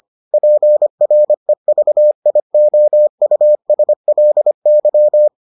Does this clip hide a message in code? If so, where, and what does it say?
Yes: Morse code from 0.3 to 5.3 s, heard as a keyed tone, about 25 wpm, 600 Hz. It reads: PREVIOUSLY